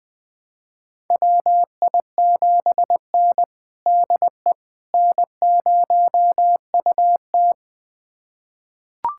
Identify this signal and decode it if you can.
Morse code, heard as a keyed tone, about 20 wpm, 700 Hz. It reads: WI7N DE N0UT
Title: USA Callsign DE Callsign - 4 Characters - 20wpm